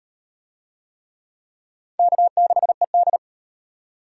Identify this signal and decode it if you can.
Morse code, heard as a keyed tone, about 35 wpm, 700 Hz. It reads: K6ED